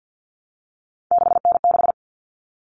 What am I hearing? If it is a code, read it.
Morse code, heard as a keyed tone, about 48 wpm, 700 Hz. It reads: 6N6
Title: All Cut Numbers - 3-Digit 50wpm